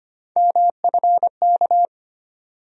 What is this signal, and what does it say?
Morse code, heard as a keyed tone, about 25 wpm, 700 Hz. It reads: MFK
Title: USA Call Sign Postfixes - 3 Letters 25wpm